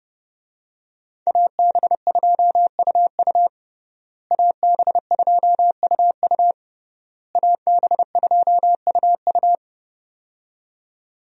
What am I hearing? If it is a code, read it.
Morse code, heard as a keyed tone, about 30 wpm, 700 Hz. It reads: AB2UU AB2UU AB2UU
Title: USA Call Signs - 5 Characters - Repeated 3x - 30wpm